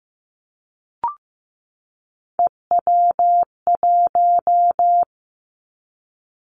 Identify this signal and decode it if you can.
Morse code, heard as a keyed tone, about 15 wpm, 700 Hz. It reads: EW1
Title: Three Challenging Letter-Number Combinations 15wpm